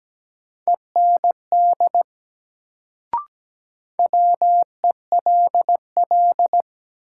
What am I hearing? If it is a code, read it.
Morse code, heard as a keyed tone, about 17 wpm, 700 Hz. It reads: END WELL
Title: Sentences from Top 200 Words - New Word Drill - 17wpm